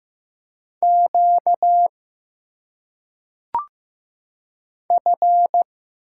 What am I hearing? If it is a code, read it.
Morse code, heard as a keyed tone, about 15 wpm, 700 Hz. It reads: Q F
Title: Single Letters - Phonetic Pronunciation 15wpm